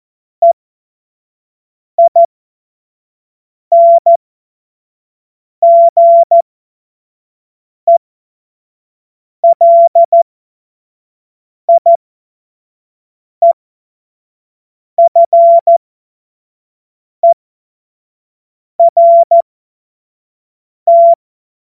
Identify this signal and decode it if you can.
Morse code, heard as a keyed tone, about 14 wpm, spaced as for 5 wpm, 680 Hz. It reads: EINGELIEFERT
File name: Morsekurs / Witze / Witz 04. Juli 2023 25 F70